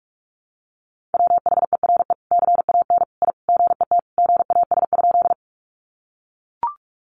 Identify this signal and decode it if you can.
Morse code, heard as a keyed tone, about 44 wpm, 700 Hz. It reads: WHERE CAN I GET GAS?
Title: Sentences from Top 1100 Words 45wpm